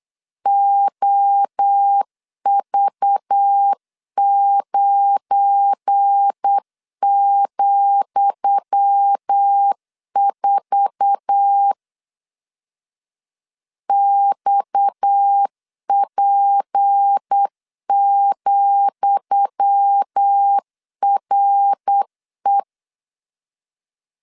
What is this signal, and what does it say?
Morse code, heard as a keyed tone, about 8 wpm, 800 Hz. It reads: OV9,4 XP,RE
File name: LEZIONI / 10 SIGNS / 10D COMA 8wpm 3749